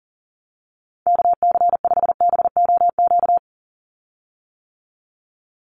Morse code, heard as a keyed tone, about 40 wpm, 700 Hz. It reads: KC5BOQ
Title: USA Call Signs 40wpm